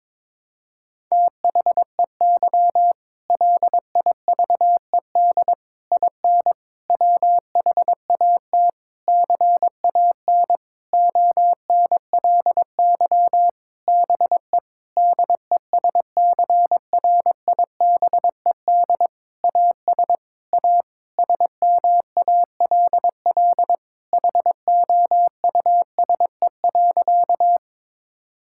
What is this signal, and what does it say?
Morse code, heard as a keyed tone, about 22 wpm, 700 Hz. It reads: THEY LIVED IN WHAT CAN ONLY BE DESCRIBED AS A SMALL HOUSE.
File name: practice / 22wpm / Sentences with 8-Word Phrases - 22wpm